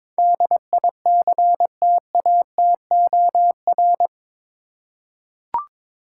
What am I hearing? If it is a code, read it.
Morse code, heard as a keyed tone, about 22 wpm, 700 Hz. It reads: DICTATOR
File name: practice / 22wpm / Eight Letter Words - 22wpm